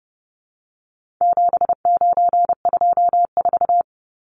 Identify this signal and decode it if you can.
Morse code, heard as a keyed tone, about 30 wpm, 700 Hz. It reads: 7924
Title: Numbers - 4-Digit 30wpm